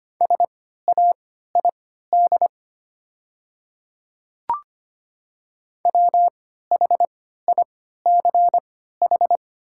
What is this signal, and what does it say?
Morse code, heard as a keyed tone, about 25 wpm, spaced as for 14 wpm, 690 Hz. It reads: SAID WHICH
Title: 3x Character Spacing - Top 100 Words - 25wpm